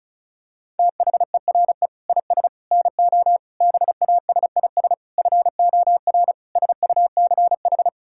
Morse code, heard as a keyed tone, about 36 wpm, 700 Hz. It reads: THERE IS NO BASIS FOR SUCH
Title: Sentences from Top 1000 Words - Review Part 5 - 35wpm